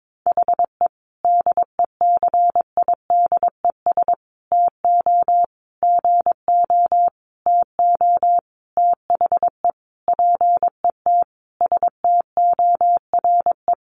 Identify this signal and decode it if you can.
Morse code, heard as a keyed tone, about 22 wpm, 700 Hz. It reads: HE DECIDES TO GO TO THE PET STORE
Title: Easy Phrases 22wpm